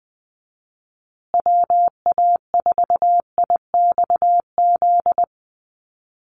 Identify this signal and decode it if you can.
Morse code, heard as a keyed tone, about 20 wpm, 700 Hz. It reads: WA4IXZ
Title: USA Call Signs Repeated 3x 20wpm